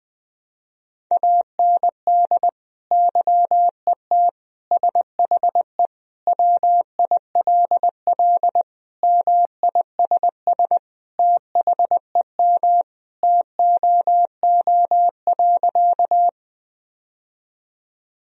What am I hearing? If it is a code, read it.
Morse code, heard as a keyed tone, about 21 wpm, 700 Hz. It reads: AND YET SHE WILL MISS THEM TOO.
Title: Sentences from Top 1000 Words - Review Part 3 - 20wpm